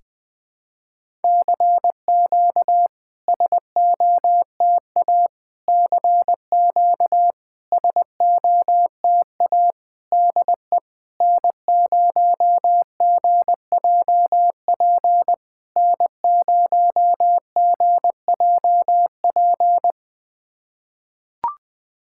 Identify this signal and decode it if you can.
Morse code, heard as a keyed tone, about 20 wpm, 690 Hz. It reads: CQ SOTA CQ SOTA DE N0GJP N0GJP
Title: SOTA QSO Practice 20wpm